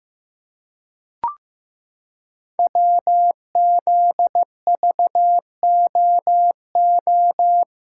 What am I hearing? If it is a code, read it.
Morse code, heard as a keyed tone, about 15 wpm, 700 Hz. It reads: WZVOO
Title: Code Groups - 5 Letter:Numbers - 15wpm